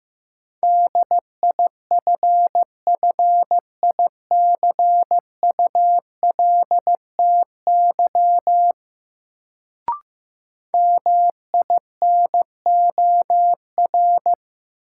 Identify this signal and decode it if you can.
Morse code, heard as a keyed tone, about 15 wpm, 700 Hz. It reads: DIFFICULTY MINOR
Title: Sentences from Top 1500 Words - New Word Drill Alt - 15wpm